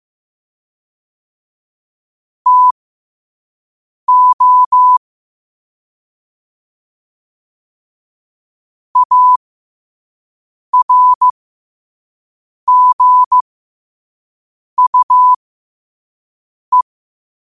Morse code, heard as a keyed tone, about 15 wpm, spaced as for 5 wpm, 1000 Hz. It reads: TO ARGUE
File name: qotdfiles / 5wpm